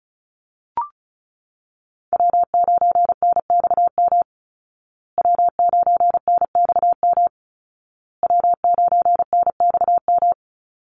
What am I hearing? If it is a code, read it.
Morse code, heard as a keyed tone, about 34 wpm, 700 Hz. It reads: W9NXM W9NXM W9NXM
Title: Call Signs Repeated 2x - 35wpm